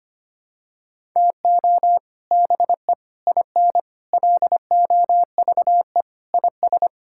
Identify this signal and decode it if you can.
Morse code, heard as a keyed tone, about 25 wpm, 700 Hz. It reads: TO BE IN LOVE IS
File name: practice / 25wpm / Sentences with 3-Word Phrases - 25wpm